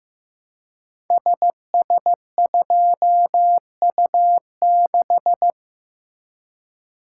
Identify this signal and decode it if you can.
Morse code, heard as a keyed tone, about 15 wpm, 700 Hz. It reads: SS2U6
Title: Code Groups - 5 Letter:Numbers - 15wpm